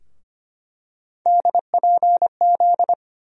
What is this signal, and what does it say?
Morse code, heard as a keyed tone, about 25 wpm, 700 Hz. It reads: DPZ
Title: USA Call Sign Postfixes - 3 Letters 25wpm